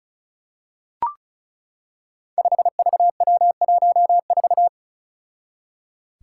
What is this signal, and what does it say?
Morse code, heard as a keyed tone, about 36 wpm, 700 Hz. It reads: 5VW14
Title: Five Challenging Letter-Number Combinations 35wpm